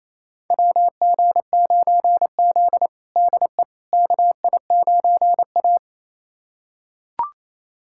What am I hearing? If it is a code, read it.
Morse code, heard as a keyed tone, about 28 wpm, 700 Hz. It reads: WG9Z DE KI9A
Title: USA Callsign DE Callsign - 4 Characters - 28wpm